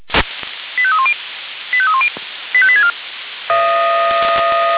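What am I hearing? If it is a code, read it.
Five-tone selective call (ZVEI), heard as quick runs of five tones, tones of about 70 ms. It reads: 85310 85310 85864
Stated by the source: ZVEI five-tone selective call received on an SDR.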